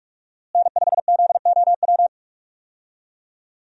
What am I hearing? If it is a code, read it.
Morse code, heard as a keyed tone, about 42 wpm, 700 Hz. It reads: N5ZOW
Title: USA Call Signs - 5 Characters - Repeated 2x - 45wpm